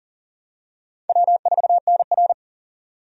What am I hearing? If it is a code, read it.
Morse code, heard as a keyed tone, about 39 wpm, 700 Hz. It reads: W4NR
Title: USA Call Signs - 4 Characters - 40wpm